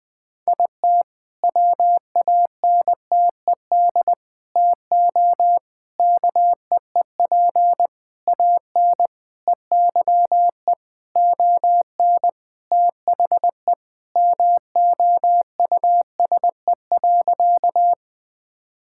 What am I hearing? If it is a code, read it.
Morse code, heard as a keyed tone, about 20 wpm, 700 Hz. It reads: IT WANTED TO KEEP AN EYE ON THE MOUSE.
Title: All-Age Appropriate Jokes - 20wpm